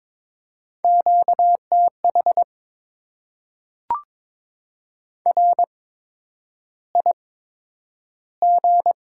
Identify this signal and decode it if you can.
Morse code, heard as a keyed tone, about 22 wpm, spaced as for 9 wpm, 700 Hz. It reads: QTH RIG
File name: Top 20 QSO Elements - 8x Character Spacing - 22wpm